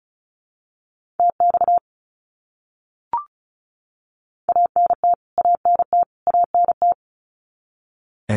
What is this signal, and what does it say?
Morse code, heard as a keyed tone, about 34 wpm, 700 Hz. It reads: TX ANT ANT ANT
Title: Abbreviations Repeated 3x - 35wpm